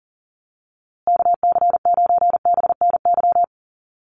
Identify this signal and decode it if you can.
Morse code, heard as a keyed tone, about 40 wpm, 700 Hz. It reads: KC9BNY
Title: USA Call Signs Repeated 2x 40wpm